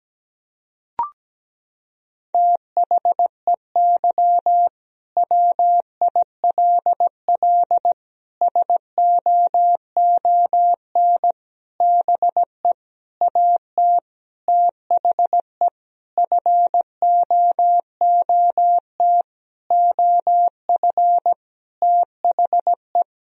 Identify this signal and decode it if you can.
Morse code, heard as a keyed tone, about 17 wpm, 700 Hz. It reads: THEY WILL SOON BE AT THE FOOT OF THE
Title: Sentences from Top 1000 Words - Review Part 4 - 17wpm